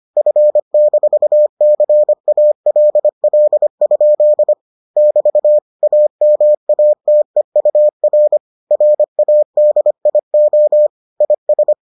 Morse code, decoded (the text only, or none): F-CALL? <BT> AMATEUR RADIO IS